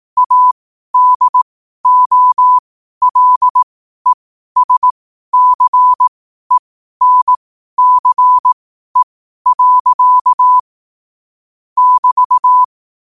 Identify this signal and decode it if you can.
Morse code, heard as a keyed tone, about 18 wpm, spaced as for 12 wpm, 1000 Hz. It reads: ADOLESCENCE. <BT>
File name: qotdfiles / 12wpm